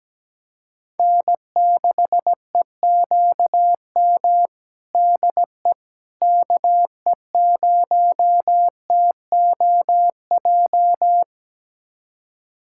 Morse code, heard as a keyed tone, about 17 wpm, 700 Hz. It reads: N6EQM DE KE0TOJ
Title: USA Callsign DE Callsign 17wpm